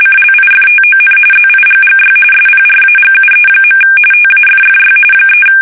RTTY, about 45.45 baud. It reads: CLIFTON ZIGGY SPRAGUE. TAFFY 3
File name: fsk850 / battleshipnj